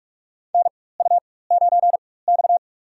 Morse code, heard as a keyed tone, about 47 wpm, spaced as for 21 wpm, 700 Hz. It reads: NU9X
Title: USA Call Signs - 4x Character Spacing - 45wpm